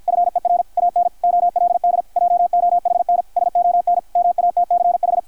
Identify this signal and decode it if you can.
Morse code, heard as a keyed tone, about 52 wpm, 660 Hz, with radio noise. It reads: FER AN OLD JOHNSON MATCH
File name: archived / CRSnet